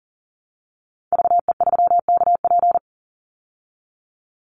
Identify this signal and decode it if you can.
Morse code, heard as a keyed tone, about 40 wpm, 700 Hz. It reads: VE3KP
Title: Call Signs Repeated 3x - 40wpm